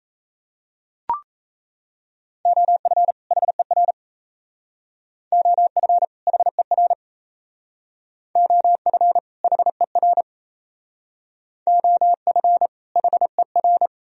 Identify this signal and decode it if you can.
Morse code, heard as a keyed tone, about 35 wpm, 690 Hz. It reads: OF HER OF HER OF HER OF HER
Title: Speed-Racing - Sets of 2 Words 28wpm